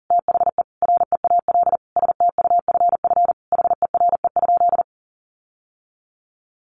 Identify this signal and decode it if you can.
Morse code, heard as a keyed tone, about 40 wpm, 700 Hz. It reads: THE REAL STUFF HERE?
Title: Sentences from Top 1400 Words 40wpm